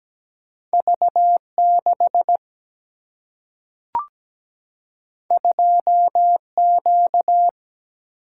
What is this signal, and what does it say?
Morse code, heard as a keyed tone, about 17 wpm, 700 Hz. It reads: V6 2Q
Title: Single-Letter Single-Number 17wpm